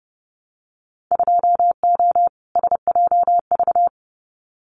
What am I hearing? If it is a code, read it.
Morse code, heard as a keyed tone, about 30 wpm, 700 Hz. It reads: 2O SJV
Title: Winter Field Day Exchanges Encore 30wpm